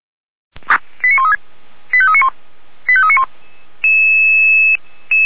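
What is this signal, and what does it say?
Five-tone selective call (ZVEI), heard as quick runs of five tones, tones of about 70 ms. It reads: 89136 85391 85391
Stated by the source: ZVEI five-tone selective call received by SDR, archived by the Orlamünde fire brigade (Feuerwehr Orlamünde).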